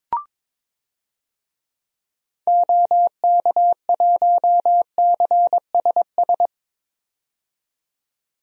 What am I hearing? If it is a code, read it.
Morse code, heard as a keyed tone, about 22 wpm, 700 Hz. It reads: OK1CSS
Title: DXCC-100 Call Signs - 6 Characters - 22wpm